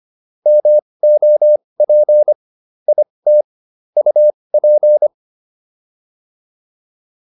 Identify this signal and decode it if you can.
Morse code, heard as a keyed tone, about 25 wpm, spaced as for 20 wpm, 600 Hz. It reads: MOP IT UP